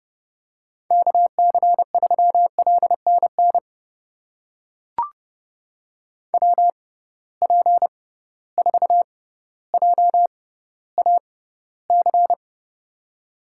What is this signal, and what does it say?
Morse code, heard as a keyed tone, about 30 wpm, spaced as for 15 wpm, 700 Hz. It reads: KC3LNN WP4JAC
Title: USA Call Signs - 6 Characters - 6x Character Spacing - 30wpm